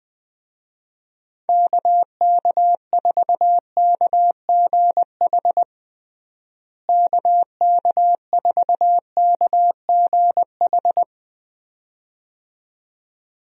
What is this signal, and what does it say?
Morse code, heard as a keyed tone, about 20 wpm, 700 Hz. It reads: KK4KGH KK4KGH
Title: USA Call Signs - 6 Characters - Repeated 2x - 20wpm